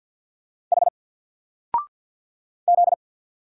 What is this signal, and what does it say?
Morse code, heard as a keyed tone, about 48 wpm, 700 Hz. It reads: H Z
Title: Single Letter-Number Alt 50wpm